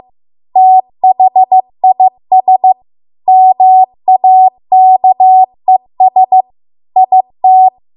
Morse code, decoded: THIS MAKES IT

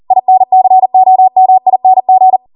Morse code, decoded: INCOMING

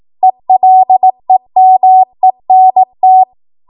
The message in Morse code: ELEMENT

